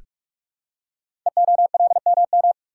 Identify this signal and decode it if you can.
Morse code, heard as a keyed tone, about 44 wpm, 700 Hz. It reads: EOLMM
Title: Code Groups - 5 Letters - 45wpm